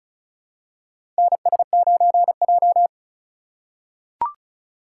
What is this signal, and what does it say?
Morse code, heard as a keyed tone, about 36 wpm, 700 Hz. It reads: NS9J
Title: USA Call Signs - 4x Character Spacing - 35wpm